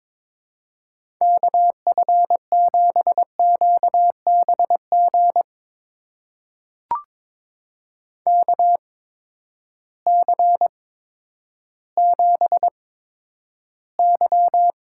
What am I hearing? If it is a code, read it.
Morse code, heard as a keyed tone, about 22 wpm, spaced as for 11 wpm, 700 Hz. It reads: KF7QBG KC7Y
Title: USA Call Signs - 6 Characters - 8x Character Spacing - 22wpm